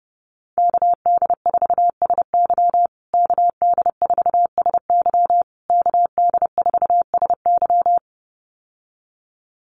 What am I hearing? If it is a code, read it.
Morse code, heard as a keyed tone, about 30 wpm, 700 Hz. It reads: KD4SY KD4SY KD4SY
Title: USA Call Signs Repeated 3x 30wpm